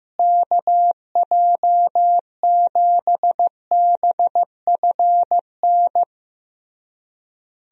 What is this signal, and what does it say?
Morse code, heard as a keyed tone, about 15 wpm, 700 Hz. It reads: KJ7BFN
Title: USA Call Signs - 6 Characters - Repeated 3x - 15wpm